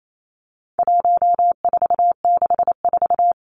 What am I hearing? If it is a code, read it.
Morse code, heard as a keyed tone, about 28 wpm, 700 Hz. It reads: 1464
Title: Numbers - 4-Digit 28wpm